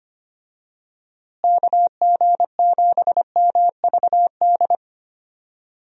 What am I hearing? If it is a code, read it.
Morse code, heard as a keyed tone, about 25 wpm, 700 Hz. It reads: KG7MVD
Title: USA Call Signs - 6 Characters - Repeated 2x - 25wpm